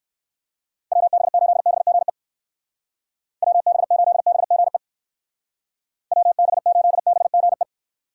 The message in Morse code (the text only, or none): WB8BGE WB8BGE WB8BGE